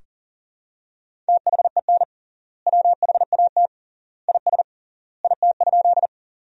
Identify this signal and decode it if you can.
Morse code, heard as a keyed tone, about 40 wpm, 700 Hz. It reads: THEN WHAT IS IT?
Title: 3x Word-Spacing - Sentences from Top 100 Words - 40wpm